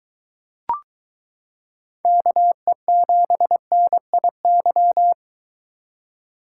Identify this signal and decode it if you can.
Morse code, heard as a keyed tone, about 23 wpm, 700 Hz. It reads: KE7NIY